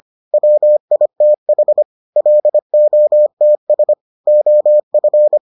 Morse code, heard as a keyed tone, about 25 wpm, 600 Hz. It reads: WITH LOTS OF